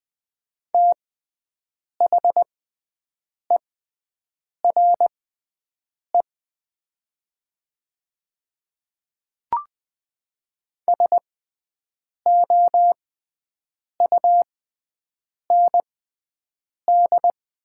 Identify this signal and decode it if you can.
Morse code, heard as a keyed tone, about 20 wpm, spaced as for 7 wpm, 700 Hz. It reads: THERE SOUND
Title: 6x Character Spacing - Top 100 Words - 20wpm